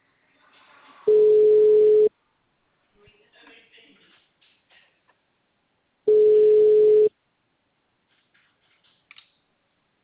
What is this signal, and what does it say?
Ringback tone (Europe, 425 Hz), with speech.